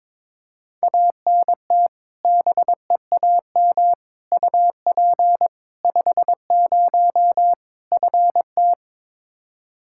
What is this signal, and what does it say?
Morse code, heard as a keyed tone, about 22 wpm, 700 Hz. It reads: ANT BEAM UP 50 FT
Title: Practice QSO - Exchange 2 - Info - 22wpm